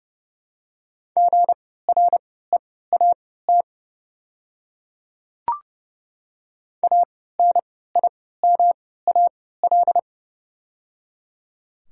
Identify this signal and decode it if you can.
Morse code, heard as a keyed tone, about 30 wpm, spaced as for 17 wpm, 700 Hz. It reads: GREAT ANIMAL
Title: 3x Character Spacing - Top 300 Words - 30wpm